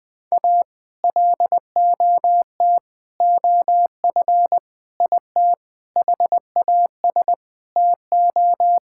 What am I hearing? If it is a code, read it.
Morse code, heard as a keyed tone, about 20 wpm, 700 Hz. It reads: A LOT OF IT HAS TO
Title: Sets of 8 Words - 20wpm